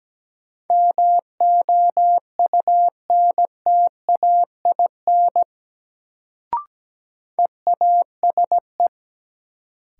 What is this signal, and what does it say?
Morse code, heard as a keyed tone, about 17 wpm, 700 Hz. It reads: MOUNTAIN EASE
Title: Sentences from Top 1000 Words - Review Part 1 - New Word Drill - 17wpm